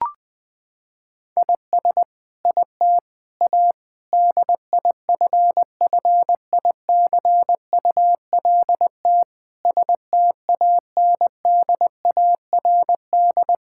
Morse code, heard as a keyed tone, about 20 wpm, 700 Hz. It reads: IS IT A DIFFICULT STANDARD
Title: Sentences from Top 1000 Words - Review Part 6 - 20wpm